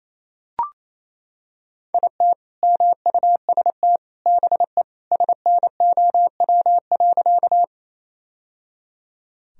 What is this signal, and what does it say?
Morse code, heard as a keyed tone, about 28 wpm, 700 Hz. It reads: IT MUST BE SNOW.
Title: Sentences from Top 1000 Words - Review Part 3 - 28wpm